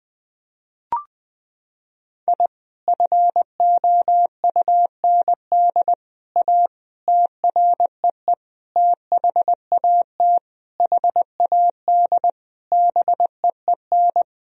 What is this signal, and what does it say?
Morse code, heard as a keyed tone, about 20 wpm, 700 Hz. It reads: I FOUND A TREE THAT HAD BEEN
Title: Sentences from Top 1000 Words - Review Part 1 - 20wpm